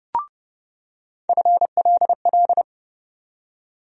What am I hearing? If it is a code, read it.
Morse code, heard as a keyed tone, about 30 wpm, 700 Hz. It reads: FLL
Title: USA Call Sign Postfixes - 3 Letters 30wpm